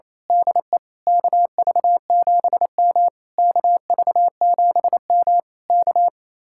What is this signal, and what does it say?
Morse code, heard as a keyed tone, about 26 wpm, 700 Hz. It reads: DE KV7M KV7M K
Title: Speed-Racing Common QSO Phrases - 28wpm